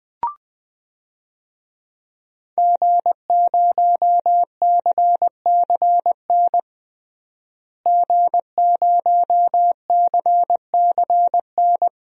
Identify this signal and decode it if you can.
Morse code, heard as a keyed tone, about 20 wpm, 700 Hz. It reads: G0CCN G0CCN
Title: DXCC-100 Call Signs - 5 Characters - Repeated 3x - 20wpm